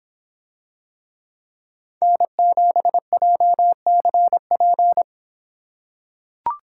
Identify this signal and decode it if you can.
Morse code, heard as a keyed tone, about 25 wpm, 700 Hz. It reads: N7JCP